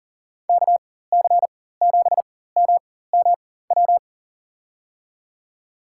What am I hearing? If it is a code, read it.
Morse code, heard as a keyed tone, about 40 wpm, spaced as for 19 wpm, 700 Hz. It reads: KC7MMW
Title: USA Call Signs - 6 Characters - 4x Character Spacing - 40wpm